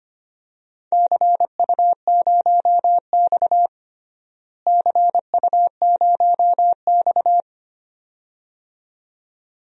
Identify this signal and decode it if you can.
Morse code, heard as a keyed tone, about 25 wpm, 700 Hz. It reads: CU0X CU0X
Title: DXCC-100 Call Signs - 4 Characters - Repeated 2x - 25wpm